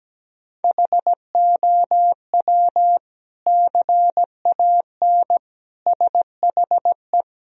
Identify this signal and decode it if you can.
Morse code, heard as a keyed tone, about 17 wpm, 700 Hz. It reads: HOW CAN SHE
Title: Sentences from Top 1000 Words - Review Part 4 - 17wpm